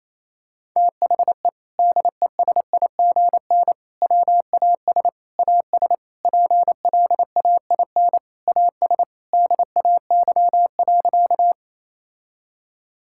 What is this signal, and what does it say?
Morse code, heard as a keyed tone, about 28 wpm, 700 Hz. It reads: THE DESIGN WAS AS PLAIN AS DAY.
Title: Sentences from Top 1000 Words - Review Part 5 - 28wpm